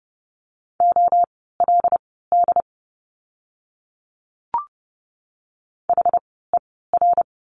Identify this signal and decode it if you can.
Morse code, heard as a keyed tone, about 30 wpm, spaced as for 17 wpm, 700 Hz. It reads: OLD HER
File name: course / 30:10wpm / 230 Lesson 37f - Letter x - Words 30wpm